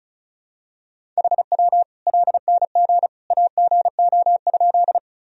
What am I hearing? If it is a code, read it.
Morse code, heard as a keyed tone, about 34 wpm, 700 Hz. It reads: HW LNG AGO?
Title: Sentences from Top 500 Words and Abbreviations 35wpm